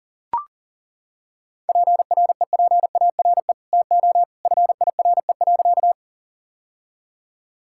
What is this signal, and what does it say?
Morse code, heard as a keyed tone, about 37 wpm, 700 Hz. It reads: PREPARE TO FIRE.
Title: Sentences from Top 1000 Words - Review Part 6 - 40wpm